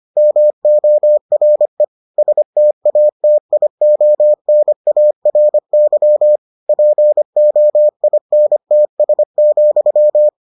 Morse code, heard as a keyed tone, about 25 wpm, 600 Hz. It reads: MORE STATIONARY POINTS,